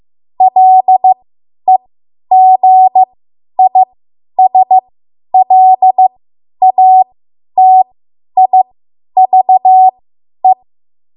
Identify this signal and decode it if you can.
Morse code, heard as a keyed tone, about 15 wpm, spaced as for 10 wpm, 750 Hz. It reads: LEGISLATIVE
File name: Morse / 10WPM